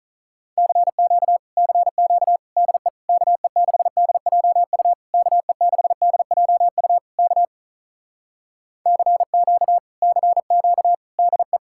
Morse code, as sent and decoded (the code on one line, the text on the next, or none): -.-. --.- / -.-. --.- / -.. / -.- . -.... -.. .--- ..- / -.- . -.... -.. .--- ..- / -.- / -.-. --.- / -.-. --.- / -.. .
CQ CQ DE KE6DJU KE6DJU K CQ CQ DE